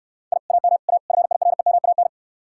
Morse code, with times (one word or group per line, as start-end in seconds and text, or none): 0.3-0.4 I
0.5-0.8 AM
0.9-1.0 A
1.1-2.1 FIREMAN